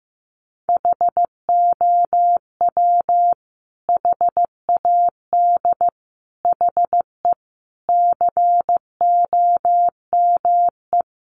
Morse code, heard as a keyed tone, about 15 wpm, 700 Hz. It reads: HOW HAD HE COME
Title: Sentences from Top 1300 Words 15wpm